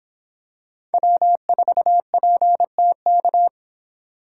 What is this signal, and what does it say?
Morse code, heard as a keyed tone, about 26 wpm, 700 Hz. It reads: W4PTK